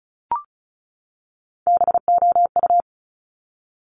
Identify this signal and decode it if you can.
Morse code, heard as a keyed tone, about 36 wpm, 690 Hz. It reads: BOU
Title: Top Three Letter Combinations in Words 35wpm